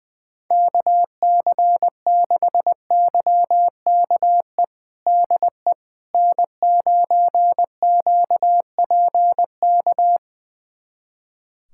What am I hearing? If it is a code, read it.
Morse code, heard as a keyed tone, about 20 wpm, 700 Hz. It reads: KC6YKE DE N9QPK
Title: USA Callsign DE Callsign 20wpm